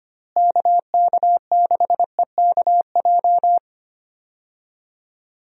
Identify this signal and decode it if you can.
Morse code, heard as a keyed tone, about 25 wpm, 700 Hz. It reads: KK6EKJ
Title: USA Call Signs - 6 Characters - Repeated 3x - 25wpm